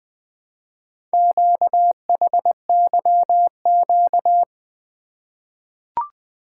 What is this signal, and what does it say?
Morse code, heard as a keyed tone, about 20 wpm, 700 Hz. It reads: QHYQ
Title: Four Letter-Number Combinations - 20wpm